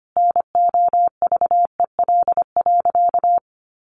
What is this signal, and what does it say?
Morse code, heard as a keyed tone, about 25 wpm, 700 Hz. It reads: NOVEL.
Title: Sentences from Top 1300 Words 25wpm